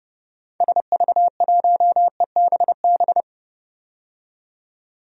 Morse code, heard as a keyed tone, about 30 wpm, 700 Hz. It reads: SV1EBB
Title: DXCC-100 Call Signs - 6 Characters - 30wpm